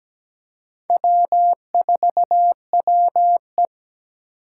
Morse code, heard as a keyed tone, about 17 wpm, 700 Hz. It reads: W4WE
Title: Four Challenging Letter-Number Combinations 17wpm